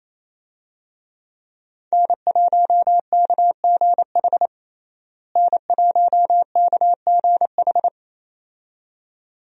Morse code, heard as a keyed tone, about 28 wpm, 700 Hz. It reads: N1KGH N1KGH
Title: USA Call Signs - 5 Characters - Repeated 2x - 28wpm